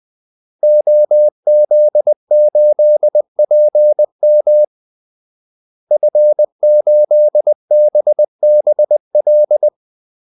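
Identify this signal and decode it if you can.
Morse code, heard as a keyed tone, about 20 wpm, 600 Hz. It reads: OZ8PM F8BBL
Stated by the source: Morse call-sign practice at 20 wpm.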